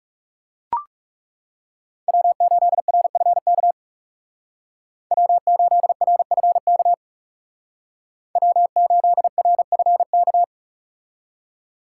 Morse code, heard as a keyed tone, about 40 wpm, 700 Hz. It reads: W8RFK W8RFK W8RFK